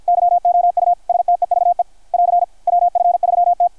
Morse code, heard as a keyed tone, about 51 wpm, 690 Hz. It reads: YOU STEVE <AR> WF3T
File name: archived / CRSnet